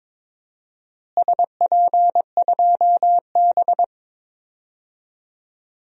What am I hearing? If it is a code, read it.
Morse code, heard as a keyed tone, about 22 wpm, 700 Hz. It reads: SP2B
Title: DXCC-100 Call Signs - 4 Characters - Repeated 2x - 22wpm